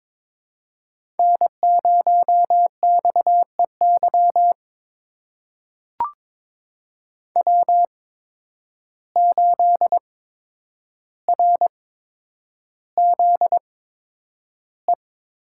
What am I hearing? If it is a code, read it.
Morse code, heard as a keyed tone, about 22 wpm, spaced as for 9 wpm, 700 Hz. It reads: N0XEY W8RZE
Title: USA Call Signs - 5 Characters - 8x Character Spacing - 22wpm